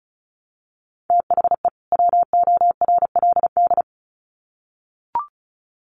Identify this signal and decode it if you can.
Morse code, heard as a keyed tone, about 34 wpm, 700 Hz. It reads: THE WORLD
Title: Sets of 2 Words 35wpm